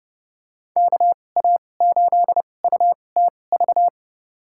Morse code, heard as a keyed tone, about 30 wpm, spaced as for 22 wpm, 700 Hz. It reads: KA8UTV
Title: USA Call Signs - 6 Characters - 2x Character Spacing - 30wpm